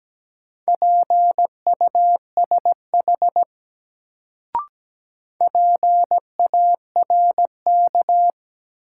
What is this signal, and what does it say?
Morse code, heard as a keyed tone, about 17 wpm, 700 Hz. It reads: PUSH PARK
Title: Sentences from Top 1300 Words - New Word Drill Alt - 17wpm